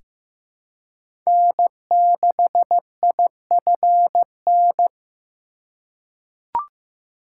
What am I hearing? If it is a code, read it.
Morse code, heard as a keyed tone, about 15 wpm, 700 Hz. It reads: N6IFN
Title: USA Call Signs - 5 Characters - 4x Character Spacing - 15wpm